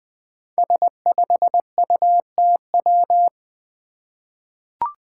Morse code, heard as a keyed tone, about 20 wpm, 700 Hz. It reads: S5UTW
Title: Five Challenging Letter-Number Combinations 20wpm